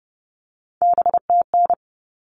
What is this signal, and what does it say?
Morse code, heard as a keyed tone, about 30 wpm, 700 Hz. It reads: BTN
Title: All Cut Numbers - 3-Digit 30wpm